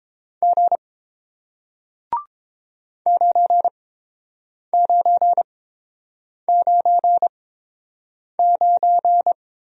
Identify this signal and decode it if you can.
Morse code, heard as a keyed tone, about 30 wpm, spaced as for 7 wpm, 700 Hz. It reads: G 9999